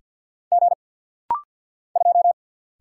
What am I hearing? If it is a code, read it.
Morse code, heard as a keyed tone, about 48 wpm, 700 Hz. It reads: G 2
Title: Single Letter-Number - Rapid-Fire 50wpm